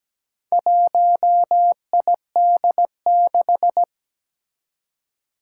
Morse code, heard as a keyed tone, about 17 wpm, 700 Hz. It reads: 1ID6
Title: Four Challenging Letter-Number Combinations 17wpm